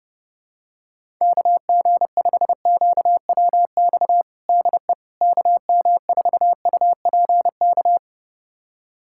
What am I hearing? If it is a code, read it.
Morse code, heard as a keyed tone, about 30 wpm, 700 Hz. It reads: KG5QWX DE KM4UPK
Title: USA Callsign DE Callsign - 6 Characters - 30wpm